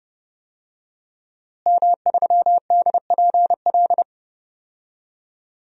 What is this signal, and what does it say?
Morse code, heard as a keyed tone, about 30 wpm, 700 Hz. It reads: M3DPL
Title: DXCC-100 Call Signs - 5 Characters - Repeated 2x - 30wpm